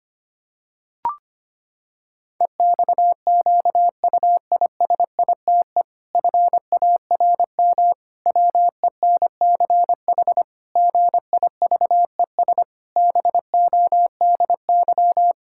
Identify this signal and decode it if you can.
Morse code, heard as a keyed tone, about 25 wpm, 700 Hz. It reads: EXQUISITE FARM WENCH GIVES BODY
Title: Pangram Challenge v2 - 25wpm